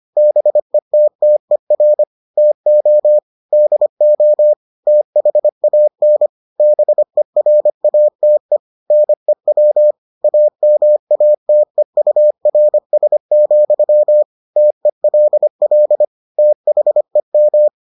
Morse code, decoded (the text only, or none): BETTER TO DO THAN BERATE NEW AMATEURS, TELL THEM